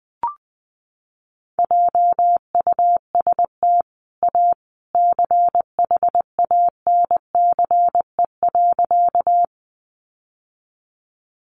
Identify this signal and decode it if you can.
Morse code, heard as a keyed tone, about 20 wpm, 700 Hz. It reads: JUST A CHANCE.
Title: Sentences from Top 800 Words 20wpm